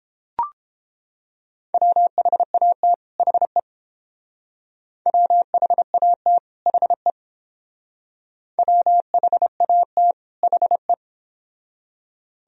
Morse code, heard as a keyed tone, about 29 wpm, 700 Hz. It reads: WHAT HE WHAT HE WHAT HE